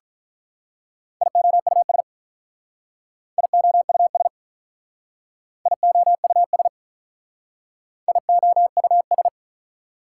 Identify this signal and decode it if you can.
Morse code, heard as a keyed tone, about 44 wpm, 700 Hz. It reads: IOUS IOUS IOUS IOUS